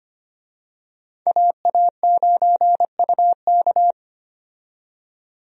Morse code, heard as a keyed tone, about 25 wpm, 700 Hz. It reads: AA9UK
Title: USA Call Signs - 5 Characters - 25wpm